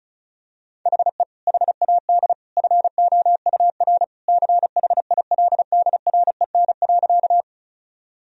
Morse code, heard as a keyed tone, about 36 wpm, 700 Hz. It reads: HE HAD FOUR CHILDREN.
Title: Sentences from Top 1000 Words - Review Part 1 - 35wpm